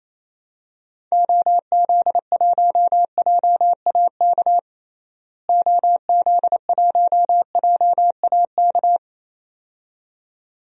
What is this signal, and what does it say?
Morse code, heard as a keyed tone, about 27 wpm, 700 Hz. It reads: OZ1JAK OZ1JAK
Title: DXCC-100 Call Signs - 6 Characters - Repeated 2x - 28wpm